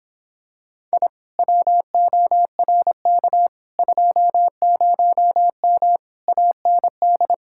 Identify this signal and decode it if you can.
Morse code, heard as a keyed tone, about 26 wpm, 700 Hz. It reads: I WORK 20M AND